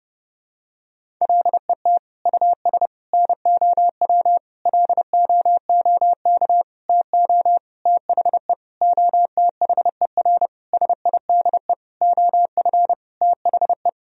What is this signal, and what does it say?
Morse code, heard as a keyed tone, about 30 wpm, 700 Hz. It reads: LET US NOW LOOK TO THE OTHER SIDE OF THE
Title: Sentences from Top 1000 Words - Review Part 1 - 30wpm